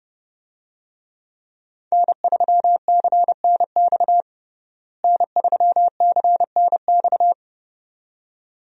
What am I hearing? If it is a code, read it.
Morse code, heard as a keyed tone, about 30 wpm, 700 Hz. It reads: N3CNX N3CNX
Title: USA Call Signs - 5 Characters - Repeated 2x - 30wpm